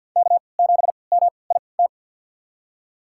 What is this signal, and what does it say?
Morse code, heard as a keyed tone, about 53 wpm, spaced as for 29 wpm, 700 Hz. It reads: K7MIT
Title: USA Call Signs - 5 Characters - 3x Character Spacing - 50wpm